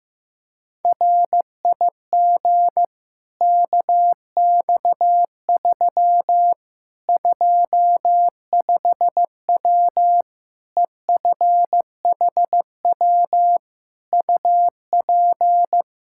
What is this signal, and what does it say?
Morse code, heard as a keyed tone, about 15 wpm, 700 Hz. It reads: RIG KX3 25W EFHW UP
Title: Practice QSO - Exchange 2 - Info - 15wpm